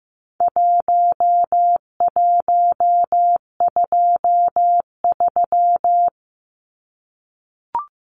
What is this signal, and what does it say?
Morse code, heard as a keyed tone, about 15 wpm, 700 Hz. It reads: 1123
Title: Numbers - 4-Digit 15wpm